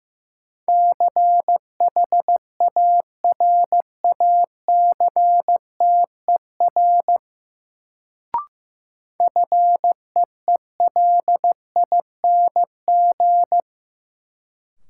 Morse code, heard as a keyed tone, about 15 wpm, 700 Hz. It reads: CHARACTER FEELING
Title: Sentences from Top 1200 Words - New Word Drill Alt - 15wpm